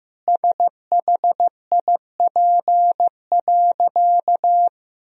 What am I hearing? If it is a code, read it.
Morse code, heard as a keyed tone, about 15 wpm, 700 Hz. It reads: SHIP.
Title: Sentences from Top 1000 Words - Review Part 2 - 15wpm